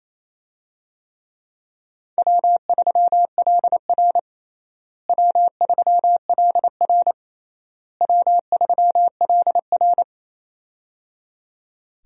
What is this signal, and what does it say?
Morse code, heard as a keyed tone, about 28 wpm, 700 Hz. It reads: W3LR W3LR W3LR
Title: USA Call Signs - 4 Characters - Repeated 3x - 28wpm